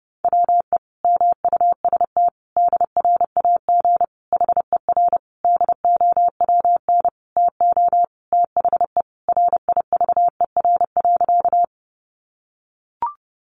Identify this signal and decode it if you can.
Morse code, heard as a keyed tone, about 30 wpm, 700 Hz. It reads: WE MUST DRAG HER DOWN TO THE RIVER.
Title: Sentences from Top 1500 Words 30wpm